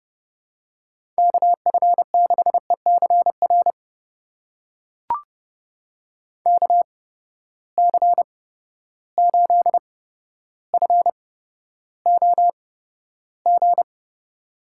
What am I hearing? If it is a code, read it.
Morse code, heard as a keyed tone, about 30 wpm, spaced as for 13 wpm, 700 Hz. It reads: KF6ECR KC8FOG
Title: USA Call Signs - 6 Characters - 8x Character Spacing - 30wpm